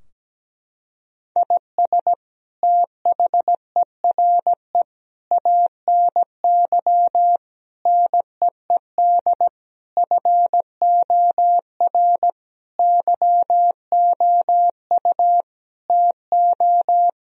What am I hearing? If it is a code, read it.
Morse code, heard as a keyed tone, about 17 wpm, 700 Hz. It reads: IS THERE ANY NEED FOR YOU TO
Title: Sentences from Top 1000 Words - Review Part 4 - 17wpm